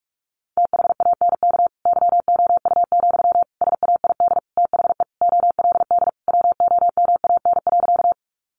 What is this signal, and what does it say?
Morse code, heard as a keyed tone, about 44 wpm, 700 Hz. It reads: THANK YOU, SAID THE OLD WOMAN.
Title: Sentences from Top 1100 Words 45wpm